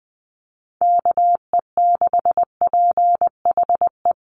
Morse code, heard as a keyed tone, about 20 wpm, 700 Hz. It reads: KE6PHE
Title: USA Call Signs Repeated 3x 20wpm